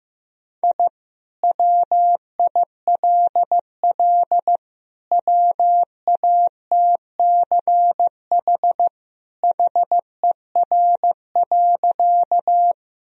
Morse code, heard as a keyed tone, about 15 wpm, 700 Hz. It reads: I WILL WATCH HER.
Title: Sentences from Top 1000 Words - Review Part 1 - 15wpm